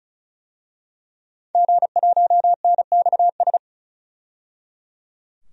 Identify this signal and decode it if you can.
Morse code, heard as a keyed tone, about 36 wpm, 700 Hz. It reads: G1NXS
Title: DXCC-100 Call Signs - 5 Characters - 35wpm